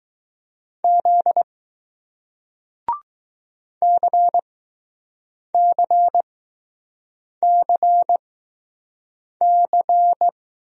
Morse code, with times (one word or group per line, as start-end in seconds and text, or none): 0.8-1.4 Z
3.8-10.3 CCCC